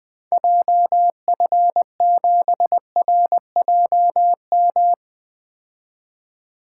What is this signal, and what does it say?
Morse code, heard as a keyed tone, about 21 wpm, 700 Hz. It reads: JF7RJM
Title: 2023 POTA Activator Call Signs - 20wpm